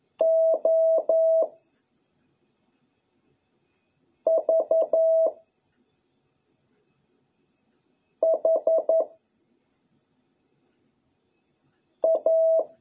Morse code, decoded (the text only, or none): O V H A